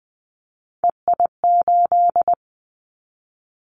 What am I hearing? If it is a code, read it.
Morse code, heard as a keyed tone, about 20 wpm, 700 Hz. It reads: EI8
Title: Callsign Prefixes for Top 50 DXCC - 20wpm